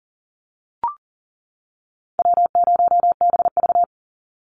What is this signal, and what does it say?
Morse code, heard as a keyed tone, about 40 wpm, 700 Hz. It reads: W0BV
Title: Easy Call Signs - 40wpm